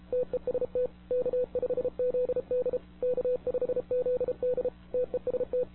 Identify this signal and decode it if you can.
Morse code, heard as a keyed tone, about 31 wpm, 500 Hz. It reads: TEST K5ZD K5ZD TEST
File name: wwcw08 / k5zd wwcw08 1700z